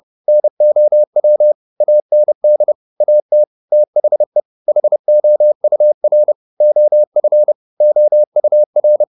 Morse code, heard as a keyed tone, about 30 wpm, 600 Hz. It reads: NOW AND AT THE HOUR OF OUR
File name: the old man and the sea / the old man and the sea 1 30wpm